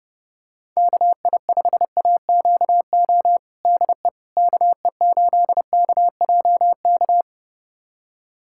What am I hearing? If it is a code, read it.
Morse code, heard as a keyed tone, about 30 wpm, 700 Hz. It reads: KI5AQO DE KE8KJK
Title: USA Callsign DE Callsign 30wpm